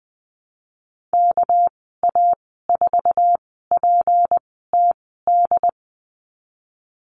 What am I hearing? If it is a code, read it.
Morse code, heard as a keyed tone, about 20 wpm, spaced as for 15 wpm, 700 Hz. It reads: KA4PTD
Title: 207 Lesson 33g - Letter j - Call Signs 20wpm